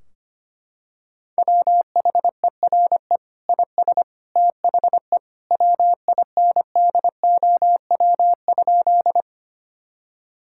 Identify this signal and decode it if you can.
Morse code, heard as a keyed tone, about 25 wpm, 700 Hz. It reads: WHERE IS THE WINDOW?
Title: Sentences from Top 1600 Words 25wpm